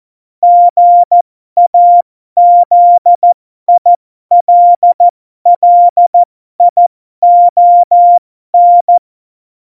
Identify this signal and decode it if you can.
Morse code, heard as a keyed tone, about 14 wpm, spaced as for 12 wpm, 700 Hz. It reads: GAZILLION
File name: Morsekurs / Witze